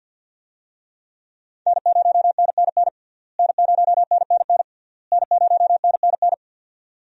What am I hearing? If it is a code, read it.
Morse code, heard as a keyed tone, about 52 wpm, 700 Hz. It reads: N0NNN N0NNN N0NNN
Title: USA Call Signs - 5 Characters - Repeated 3x - 50wpm